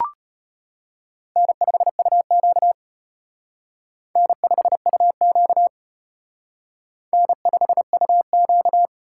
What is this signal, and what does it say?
Morse code, heard as a keyed tone, about 31 wpm, 700 Hz. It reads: N5UQ N5UQ N5UQ